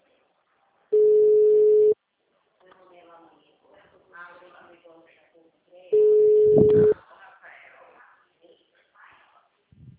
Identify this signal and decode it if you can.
Ringback tone (Europe, 425 Hz), with speech.